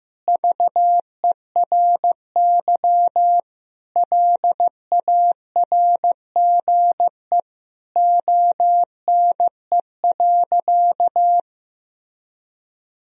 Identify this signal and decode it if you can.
Morse code, heard as a keyed tone, about 15 wpm, 700 Hz. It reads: VERY LARGE ONE.
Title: Sentences from Top 1000 Words - Review Part 1 - 15wpm